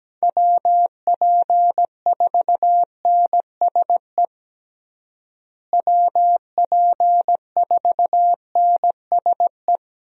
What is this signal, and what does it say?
Morse code, heard as a keyed tone, about 17 wpm, 700 Hz. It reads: WP4NSE WP4NSE
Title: USA Call Signs - 6 Characters - Repeated 2x - 17wpm